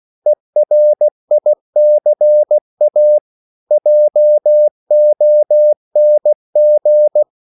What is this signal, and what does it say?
Morse code, heard as a keyed tone, about 16 wpm, 600 Hz. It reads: ERICA JONG
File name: CW / Practice / EnglishQuotations0216